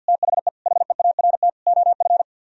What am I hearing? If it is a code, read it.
Morse code, heard as a keyed tone, about 61 wpm, 700 Hz. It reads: THE HEART OF
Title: Speed-Racing - Sets of 4 Words 50wpm